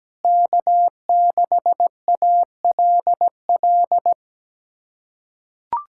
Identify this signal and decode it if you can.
Morse code, heard as a keyed tone, about 17 wpm, 700 Hz. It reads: K6ALL
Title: USA Callsign DE Callsign 17wpm